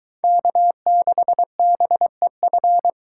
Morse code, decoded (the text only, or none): K6BEF